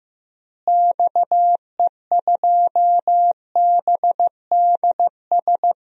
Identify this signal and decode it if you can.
Morse code, heard as a keyed tone, about 15 wpm, 700 Hz. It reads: XE2BDS
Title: DXCC-100 Call Signs - 6 Characters - Repeated 3x - 15wpm